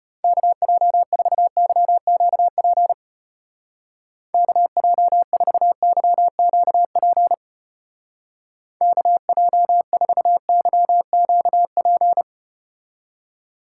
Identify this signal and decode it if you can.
Morse code, heard as a keyed tone, about 33 wpm, 700 Hz. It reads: KJ4YQP KJ4YQP KJ4YQP